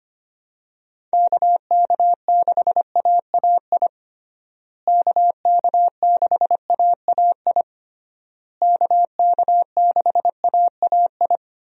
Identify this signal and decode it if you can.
Morse code, heard as a keyed tone, about 25 wpm, 700 Hz. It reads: KK6AAI KK6AAI KK6AAI
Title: USA Call Signs - 6 Characters - Repeated 3x - 25wpm